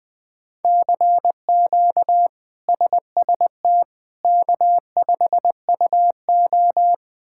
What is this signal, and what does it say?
Morse code, heard as a keyed tone, about 20 wpm, 700 Hz. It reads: CQ SST K5UO
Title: K1USN SST QSO Practice - 20wpm